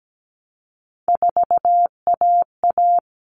Morse code, heard as a keyed tone, about 17 wpm, 700 Hz. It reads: 4AA
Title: All Cut Numbers - 3-Digit 17wpm